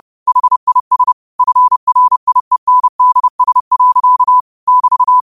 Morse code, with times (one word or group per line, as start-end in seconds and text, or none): 0.3-1.1 HIS
1.4-4.4 FRIENDS.
4.7-5.2 <BT>